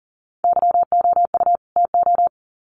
Morse code, heard as a keyed tone, about 40 wpm, 700 Hz. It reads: YOU TO
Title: Sets of 2 Words 40wpm